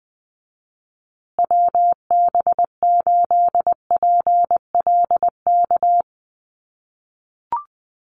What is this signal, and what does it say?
Morse code, heard as a keyed tone, about 20 wpm, 700 Hz. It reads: WB8PLK
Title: USA Call Signs Repeated 2x 20wpm